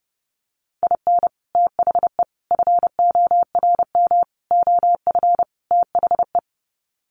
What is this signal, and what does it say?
Morse code, heard as a keyed tone, about 30 wpm, 700 Hz. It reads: IN THE FORM OF THE
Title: Sets of 5 Words Encore2 30wpm